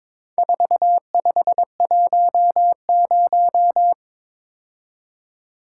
Morse code, heard as a keyed tone, about 22 wpm, 700 Hz. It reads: 4510
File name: practice / 22wpm / Four Letter-Number Combinations - 22wpm